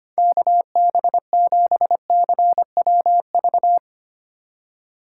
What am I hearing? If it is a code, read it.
Morse code, heard as a keyed tone, about 25 wpm, 700 Hz. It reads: KB7CWV
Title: USA Call Signs - 6 Characters - 4x Character Spacing - 25wpm